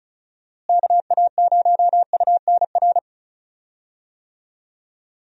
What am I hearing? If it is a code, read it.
Morse code, heard as a keyed tone, about 36 wpm, 700 Hz. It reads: KA0UNR
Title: USA Call Signs - 6 Characters - Repeated 2x - 35wpm